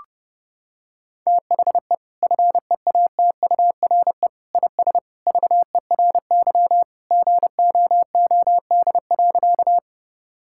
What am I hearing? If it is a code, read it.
Morse code, heard as a keyed tone, about 32 wpm, 700 Hz. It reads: THE FEATURE IS VERY GOOD.
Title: Sentences from Top 1000 Words - Review Part 5 - 30wpm